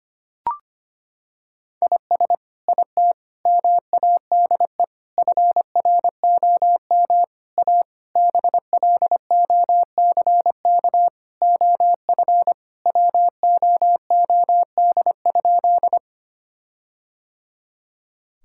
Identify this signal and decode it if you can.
Morse code, heard as a keyed tone, about 25 wpm, 700 Hz. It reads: IS IT MADE FROM A BLOCK OF WOOD?
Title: Sentences from Top 1600 Words 25wpm